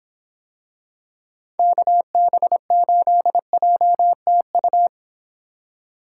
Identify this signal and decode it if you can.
Morse code, heard as a keyed tone, about 26 wpm, 700 Hz. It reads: KB8JTU